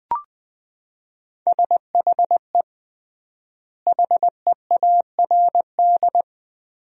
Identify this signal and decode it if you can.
Morse code, heard as a keyed tone, about 20 wpm, 700 Hz. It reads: SHE HEARD
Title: 3x Word-Spacing - Sentences from Top 500 Words - 20wpm